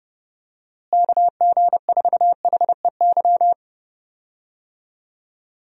Morse code, heard as a keyed tone, about 30 wpm, 700 Hz. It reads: KG4HEY
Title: USA Call Signs - 6 Characters - Repeated 3x - 30wpm